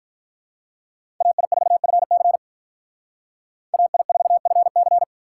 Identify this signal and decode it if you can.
Morse code, heard as a keyed tone, about 42 wpm, 700 Hz. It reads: AI4FC AI4FC
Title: Speed Racing - USA 2x2 Call Signs 35wpm